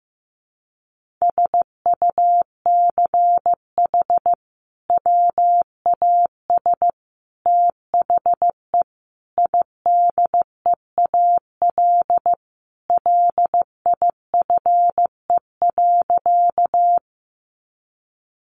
Morse code, heard as a keyed tone, about 15 wpm, 700 Hz. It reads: SUCH WAS THE IDEAL LIFE.
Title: Sentences from Top 1400 Words 15wpm